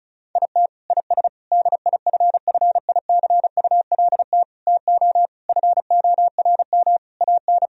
Morse code, heard as a keyed tone, about 34 wpm, 700 Hz. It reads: IT IS DIFFICULT TO FORM AN
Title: Sentences from Top 1000 Words - Review Part 1 - 35wpm